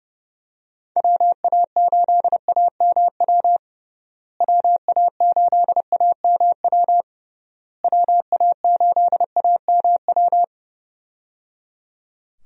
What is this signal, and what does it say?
Morse code, heard as a keyed tone, about 30 wpm, 700 Hz. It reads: WA8AMW WA8AMW WA8AMW
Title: USA Call Signs - 6 Characters - Repeated 3x - 30wpm